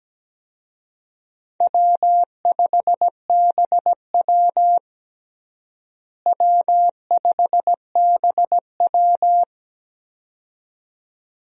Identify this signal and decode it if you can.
Morse code, heard as a keyed tone, about 17 wpm, 700 Hz. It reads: W5BW W5BW
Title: USA Call Signs - 4 Characters - Repeated 2x - 17wpm